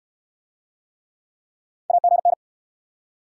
Morse code, heard as a keyed tone, about 69 wpm, 700 Hz. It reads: AGN